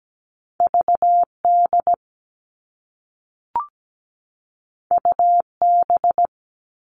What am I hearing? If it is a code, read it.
Morse code, heard as a keyed tone, about 17 wpm, 700 Hz. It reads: V D U B
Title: All Cut Numbers - 2-Digit 17wpm